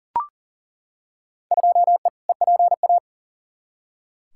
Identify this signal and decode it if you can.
Morse code, heard as a keyed tone, about 40 wpm, 700 Hz. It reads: 2E EPA
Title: Field Day Exchanges Encore - 40wpm